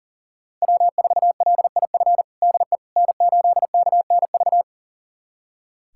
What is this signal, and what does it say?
Morse code, heard as a keyed tone, about 37 wpm, 700 Hz. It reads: W4LIF DE N8KNV
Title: USA Callsign DE Callsign - 5 Characters - 40wpm